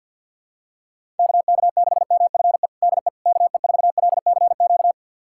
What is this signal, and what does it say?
Morse code, heard as a keyed tone, about 52 wpm, 700 Hz. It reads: KK6MFE DE KE4LCQ
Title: USA Callsign DE Callsign - 6 Characters - 50wpm